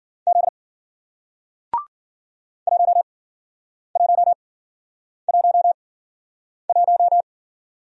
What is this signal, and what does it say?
Morse code, heard as a keyed tone, about 55 wpm, spaced as for 9 wpm, 700 Hz. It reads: Z 1111